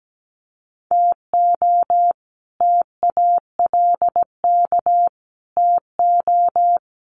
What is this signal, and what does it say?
Morse code, heard as a keyed tone, about 17 wpm, 700 Hz. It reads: TO TALK TO
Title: Sets of 4 Words 17wpm